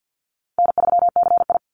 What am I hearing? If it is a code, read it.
Morse code, heard as a keyed tone, about 48 wpm, 700 Hz. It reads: N3CI
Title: Call Signs Repeated 3x - 50wpm